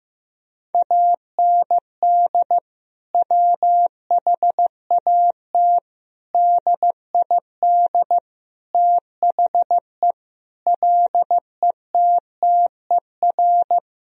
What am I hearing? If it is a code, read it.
Morse code, heard as a keyed tone, about 15 wpm, 700 Hz. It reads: AND WHAT DID THE LETTER
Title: Sentences from Top 1000 Words - Review Part 2 - 15wpm